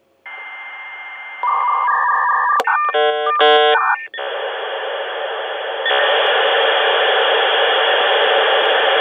Dial-up modem handshake.
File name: mf SE-7815-computer modem 1